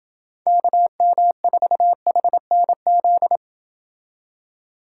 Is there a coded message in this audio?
Yes: Morse code from 0.5 to 3.4 s, heard as a keyed tone, about 28 wpm, 700 Hz. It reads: KM4HNZ